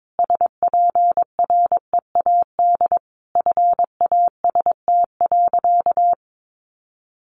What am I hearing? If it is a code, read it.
Morse code, heard as a keyed tone, about 22 wpm, 700 Hz. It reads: SPREAD FAST.
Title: Sentences from Top 1200 Words 22wpm